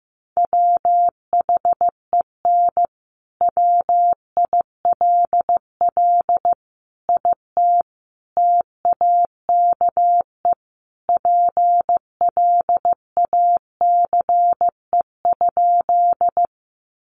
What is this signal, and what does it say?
Morse code, heard as a keyed tone, about 15 wpm, 700 Hz. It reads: WHEN WILL IT TAKE PLACE?
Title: Sentences from Top 200 Words 15wpm